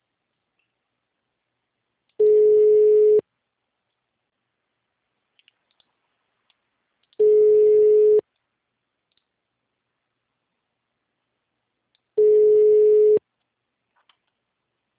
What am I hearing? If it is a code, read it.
Ringback tone (Europe, 425 Hz).